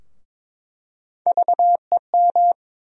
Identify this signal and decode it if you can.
Morse code, heard as a keyed tone, about 22 wpm, 700 Hz. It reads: VEM
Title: USA Call Sign Postfixes - 3 Letters 22wpm